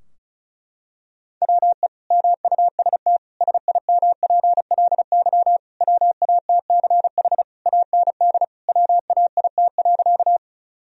Morse code, heard as a keyed tone, about 36 wpm, 700 Hz. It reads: WE MUST SIMPLY WATCH AND WAIT.
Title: Sentences from Top 1000 Words - Review Part 4 - 35wpm